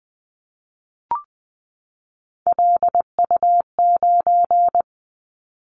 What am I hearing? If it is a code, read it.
Morse code, heard as a keyed tone, about 20 wpm, 700 Hz. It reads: LU9
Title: Callsign Prefixes for Top 100 DXCC - 20wpm